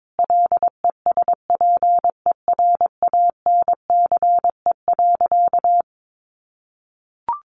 Morse code, heard as a keyed tone, about 22 wpm, 700 Hz. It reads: LESPERANCE.
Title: A Sound Of Thunder 22wpm